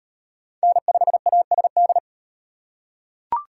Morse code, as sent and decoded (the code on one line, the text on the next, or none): -. ..... .- ... -..
N5ASD